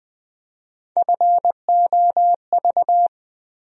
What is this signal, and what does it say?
Morse code, heard as a keyed tone, about 20 wpm, 700 Hz. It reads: FOV
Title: USA Call Sign Postfixes - 3 Letters 20wpm